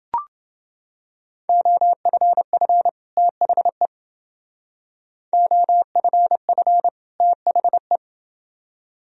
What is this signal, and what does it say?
Morse code, heard as a keyed tone, about 30 wpm, 700 Hz. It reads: OFF THE OFF THE